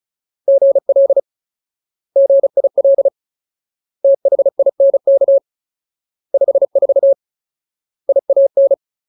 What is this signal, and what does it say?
Morse code, heard as a keyed tone, about 34 wpm, 550 Hz. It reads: GL GIL THINK 54 IAN
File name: Ultimate Word List / 35WPM Ultimate